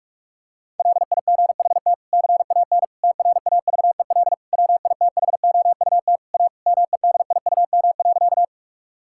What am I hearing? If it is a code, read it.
Morse code, heard as a keyed tone, about 42 wpm, 700 Hz. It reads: LIGHT CAN TRAVEL WITHOUT A MEDIUM.
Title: Sentences from Top 1000 Words - Review Part 6 - 45wpm